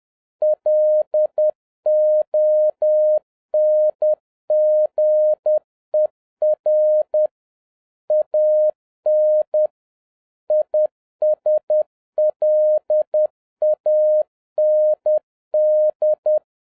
Morse code, heard as a keyed tone, about 10 wpm, 610 Hz. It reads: LONGER AN ISLAND
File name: 2011-07-mp3 / n31 THE 3.11 2 10wpm